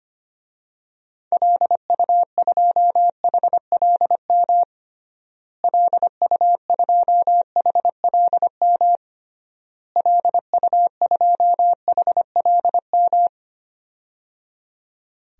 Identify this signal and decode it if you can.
Morse code, heard as a keyed tone, about 25 wpm, 700 Hz. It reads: LU2HLM LU2HLM LU2HLM
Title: DXCC-100 Call Signs - 6 Characters - Repeated 3x - 25wpm